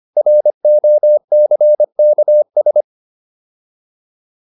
Morse code, heard as a keyed tone, about 25 wpm, 600 Hz. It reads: ROCKS